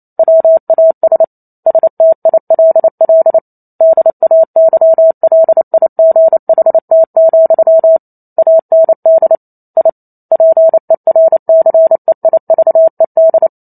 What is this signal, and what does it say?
Morse code, heard as a keyed tone, about 28 wpm, 650 Hz. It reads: WAS STILL DAYLIGHT, AND I PERCEIVED